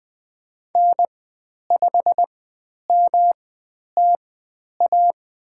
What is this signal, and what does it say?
Morse code, heard as a keyed tone, about 20 wpm, spaced as for 10 wpm, 700 Hz. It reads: N5MTA
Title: 109 Lesson 18g - Letter m - Call Signs 20wpm